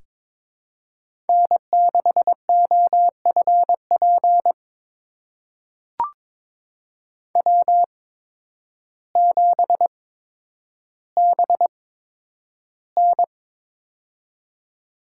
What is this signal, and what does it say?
Morse code, heard as a keyed tone, about 22 wpm, spaced as for 10 wpm, 700 Hz. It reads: N6OFP W7BN
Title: USA Call Signs - 8x Character Spacing - 22wpm